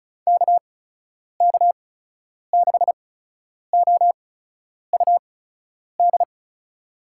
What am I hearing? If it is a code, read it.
Morse code, heard as a keyed tone, about 34 wpm, spaced as for 10 wpm, 700 Hz. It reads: KK6OUD
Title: USA Call Signs - 6 Characters - 8x Character Spacing - 35wpm